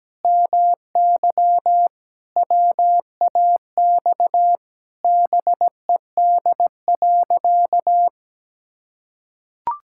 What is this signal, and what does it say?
Morse code, heard as a keyed tone, about 17 wpm, 700 Hz. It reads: MY WAX BED.
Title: Pangram Challenge v2 - 17wpm